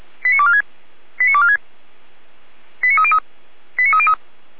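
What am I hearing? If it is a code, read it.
Five-tone selective call (ZVEI), heard as quick runs of five tones, tones of about 70 ms. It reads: 89246 89246 89303 89303
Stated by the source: ZVEI five-tone call 89169 received by SDR.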